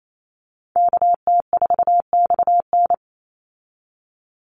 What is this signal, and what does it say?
Morse code, heard as a keyed tone, about 28 wpm, 700 Hz. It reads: KT4XN
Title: Call Signs Repeated 3x - 28wpm